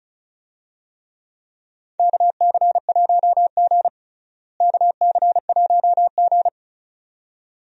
Morse code, heard as a keyed tone, about 36 wpm, 700 Hz. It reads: KC1G KC1G
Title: USA Call Signs - 4 Characters - Repeated 2x - 35wpm